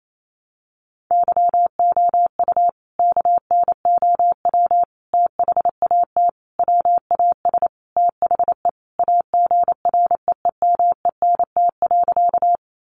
Morse code, heard as a keyed tone, about 28 wpm, 700 Hz. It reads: YOU KNOW THAT WAS THE AGREEMENT.
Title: Sentences from Top 1500 Words 28wpm